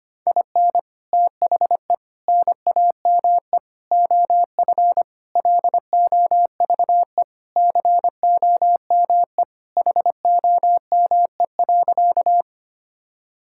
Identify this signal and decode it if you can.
Morse code, heard as a keyed tone, about 25 wpm, 700 Hz. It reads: IN THE NAME OF LOVE COME HOME.
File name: practice / 25wpm / Sentences with 3-Word Phrases - 25wpm